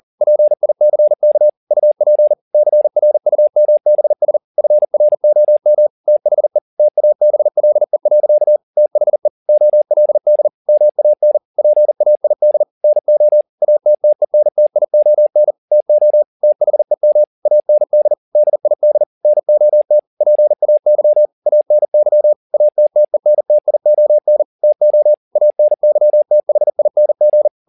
Morse code, heard as a keyed tone, about 40 wpm, 600 Hz. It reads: PICK UP CRUMBS FROM THE TABLE. THE OLD MAN PAID NO ATTENTION TO THEM AND DID NOT PAY ANY ATTENTION TO ANYTHING
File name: the old man and the sea 2 40wpm